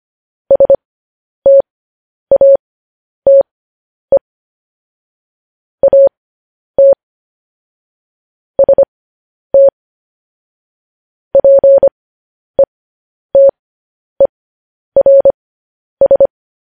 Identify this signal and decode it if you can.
Morse code, heard as a keyed tone, about 25 wpm, spaced as for 10 wpm, 550 Hz. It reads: STATE AT ST PETERS
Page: news headlines in morse code at 10 wpm